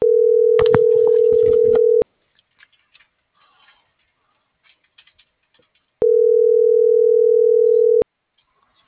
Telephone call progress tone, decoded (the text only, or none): ringback tone